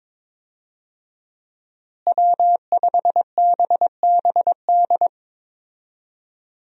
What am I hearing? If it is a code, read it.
Morse code, heard as a keyed tone, about 22 wpm, 700 Hz. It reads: W5BBD
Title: USA Call Signs - 5 Characters - Repeated 2x - 22wpm